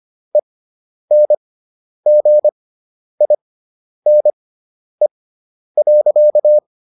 Morse code, heard as a keyed tone, about 25 wpm, spaced as for 10 wpm, 600 Hz. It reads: ENGINE.